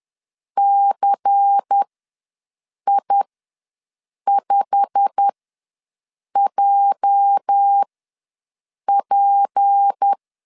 Morse code, heard as a keyed tone, about 10 wpm, spaced as for 6 wpm, 800 Hz. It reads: CI5JP
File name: LEZIONI / 8 YJ45 / 8 YJ45 ws 10wpm 3843